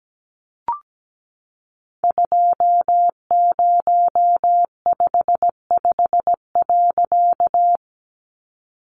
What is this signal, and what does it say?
Morse code, heard as a keyed tone, about 17 wpm, 700 Hz. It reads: 2055.
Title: A Sound Of Thunder 17wpm